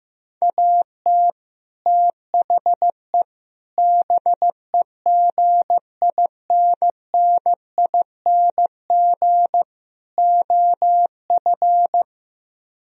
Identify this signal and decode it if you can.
Morse code, heard as a keyed tone, about 15 wpm, 700 Hz. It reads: AT THE BEGINNING OF
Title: Speed-Racing - Sets of 4 Words 15wpm